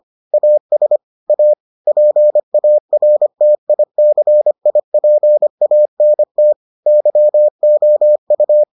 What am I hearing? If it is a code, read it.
Morse code, heard as a keyed tone, about 25 wpm, 600 Hz. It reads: AS A PARTICIPANT YOU